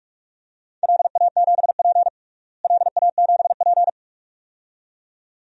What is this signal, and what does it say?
Morse code, heard as a keyed tone, about 46 wpm, 700 Hz. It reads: LA7P LA7P
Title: DXCC-100 Call Signs - 4 Characters - Repeated 2x - 45wpm